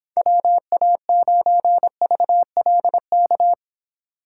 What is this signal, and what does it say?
Morse code, heard as a keyed tone, about 26 wpm, 700 Hz. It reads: WA9VLK